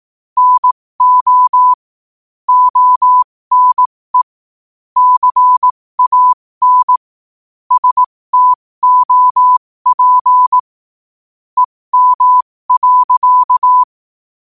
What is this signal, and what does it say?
Morse code, heard as a keyed tone, about 18 wpm, spaced as for 15 wpm, 1000 Hz. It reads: NO ONE CAN STOP EM.
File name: qotdfiles / 15wpm